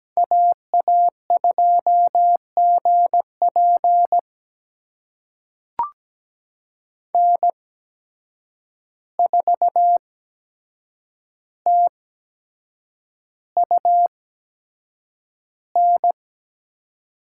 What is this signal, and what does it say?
Morse code, heard as a keyed tone, about 17 wpm, spaced as for 7 wpm, 700 Hz. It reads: AA2GP N4TUN
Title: USA Call Signs - 5 Characters - 8x Character Spacing - 17wpm